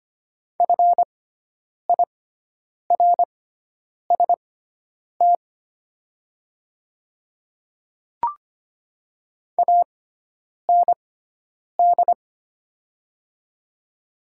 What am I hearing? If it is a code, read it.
Morse code, heard as a keyed tone, about 25 wpm, spaced as for 9 wpm, 690 Hz. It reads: FIRST AND
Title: 6x Character Spacing - Top 100 Words - 25wpm